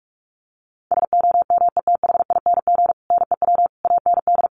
Morse code, heard as a keyed tone, about 44 wpm, 700 Hz. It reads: SOMETHING NEW AND
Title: Sentences from Top 700 Words 45wpm